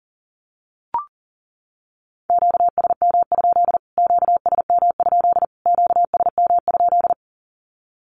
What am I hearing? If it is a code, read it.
Morse code, heard as a keyed tone, about 40 wpm, 700 Hz. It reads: QSM? QSM? QSM?
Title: Q-Codes Repeated 3x 40wpm